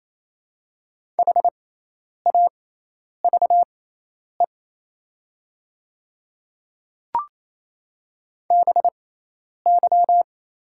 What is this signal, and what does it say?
Morse code, heard as a keyed tone, about 28 wpm, spaced as for 10 wpm, 700 Hz. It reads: HAVE BY
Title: 6x Character Spacing - Top 100 Words - 28wpm